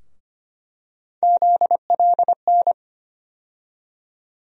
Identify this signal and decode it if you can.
Morse code, heard as a keyed tone, about 25 wpm, 700 Hz. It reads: ZLN
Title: USA Call Sign Postfixes - 3 Letters 25wpm